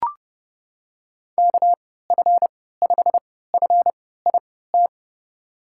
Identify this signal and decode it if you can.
Morse code, heard as a keyed tone, about 30 wpm, spaced as for 17 wpm, 700 Hz. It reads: KF5FIT
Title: USA Call Signs - 6 Characters - 3x Character Spacing - 30wpm